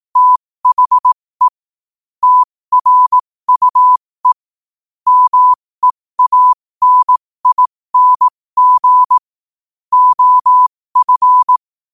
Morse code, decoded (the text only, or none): THE TRUE MEANING OF